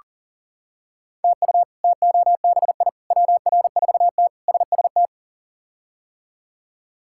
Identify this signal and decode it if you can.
Morse code, heard as a keyed tone, about 40 wpm, 700 Hz. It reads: TU TOBI WR4T SST
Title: K1USN SST QSO Practice - 40wpm